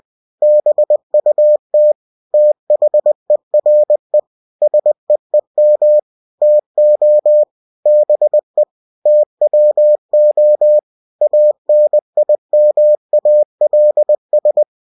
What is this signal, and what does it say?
Morse code, heard as a keyed tone, about 20 wpm, 600 Hz. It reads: BUT THERE SEEM TO BE TWO ANIMALS
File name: cw-books / winnie-the-pooh / winnie-20wpm-page-0022